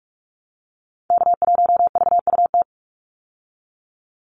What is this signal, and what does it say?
Morse code, heard as a keyed tone, about 44 wpm, 700 Hz. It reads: K1VUT
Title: Call Signs Repeated 2x - 45wpm